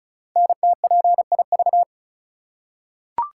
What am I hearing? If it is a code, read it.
Morse code, heard as a keyed tone, about 36 wpm, 700 Hz. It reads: NTPIV
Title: Code Groups - 5 Letter:Numbers - 35wpm